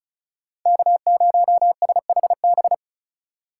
Morse code, heard as a keyed tone, about 33 wpm, 700 Hz. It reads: K0SHB
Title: Speed Racing - USA 1x3 Call Signs 35wpm